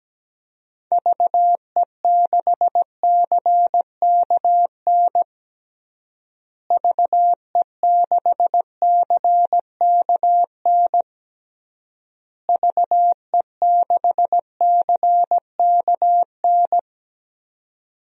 Morse code, heard as a keyed tone, about 17 wpm, 700 Hz. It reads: VE6CKN VE6CKN VE6CKN
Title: DXCC-100 Call Signs - 6 Characters - Repeated 3x - 17wpm